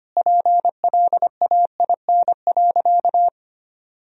Morse code, heard as a keyed tone, about 25 wpm, 700 Hz. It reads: PLAIN.
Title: Sentences from Top 400 Words and Abbreviations 25wpm